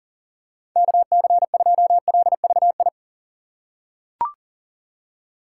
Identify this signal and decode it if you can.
Morse code, heard as a keyed tone, about 40 wpm, 700 Hz. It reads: KC2LVI
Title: USA Call Signs - 6 Characters - 2x Character Spacing - 40wpm